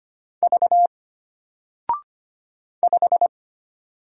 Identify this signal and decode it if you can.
Morse code, heard as a keyed tone, about 25 wpm, 700 Hz. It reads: V 5
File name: Single-Letter-Number-25wpm